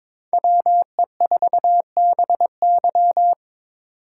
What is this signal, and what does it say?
Morse code, heard as a keyed tone, about 25 wpm, 700 Hz. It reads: WE4BY
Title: Speed Racing - USA 2x2 Call Signs 22wpm